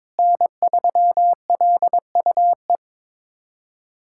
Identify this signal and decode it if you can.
Morse code, heard as a keyed tone, about 22 wpm, 700 Hz. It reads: N3LUE
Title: USA Callsign DE Callsign - 5 Characters - 22wpm